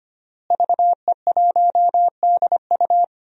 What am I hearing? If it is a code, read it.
Morse code, heard as a keyed tone, about 25 wpm, 700 Hz. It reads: VE1DU
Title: DXCC-100 Call Signs - 5 Characters - Repeated 2x - 25wpm